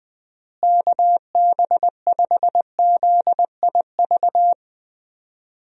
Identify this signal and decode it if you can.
Morse code, heard as a keyed tone, about 20 wpm, 700 Hz. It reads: KB5ZIV
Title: USA Call Signs - 6 Characters - Repeated 3x - 20wpm